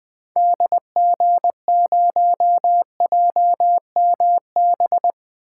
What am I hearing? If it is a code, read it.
Morse code, heard as a keyed tone, about 20 wpm, 700 Hz. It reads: DG0JMB
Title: 2023 SOTA Activator Call Signs - 20wpm